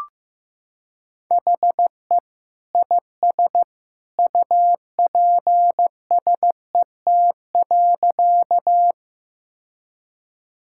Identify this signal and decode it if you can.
Morse code, heard as a keyed tone, about 14 wpm, 700 Hz. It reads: HE IS UPSET.
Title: Sentences from Top 1000 Words - Review Part 5 - 15wpm